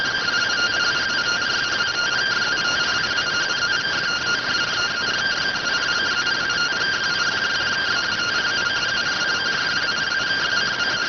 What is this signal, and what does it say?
Two-tone teleprinter signal, about 170 Hz shift.